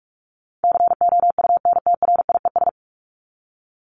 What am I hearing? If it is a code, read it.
Morse code, heard as a keyed tone, about 44 wpm, 700 Hz. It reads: COUNTRIES
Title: ES:S Suffix - Top 100 Words with Base Words 45wpm